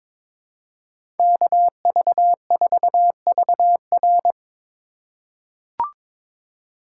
Morse code, heard as a keyed tone, about 22 wpm, 700 Hz. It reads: KV4VR
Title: USA Call Signs - 5 Characters - 3x Character Spacing - 22wpm